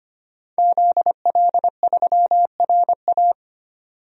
Morse code, heard as a keyed tone, about 25 wpm, 700 Hz. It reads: ZL3RA
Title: DXCC-100 Call Signs - 5 Characters - 25wpm